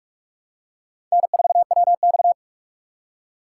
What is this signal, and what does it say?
Morse code, heard as a keyed tone, about 39 wpm, 700 Hz. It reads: N4WX